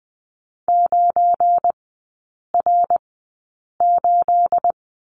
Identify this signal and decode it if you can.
Morse code, heard as a keyed tone, about 20 wpm, spaced as for 8 wpm, 700 Hz. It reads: 9R8
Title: 218 Lesson 35e - Number 0 - Three Characters 20wpm